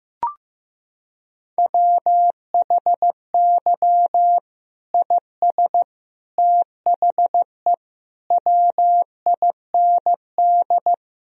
Morse code, heard as a keyed tone, about 15 wpm, 700 Hz. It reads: WHY IS THE WIND
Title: Sentences from Top 1000 Words - Review Part 2 - 15wpm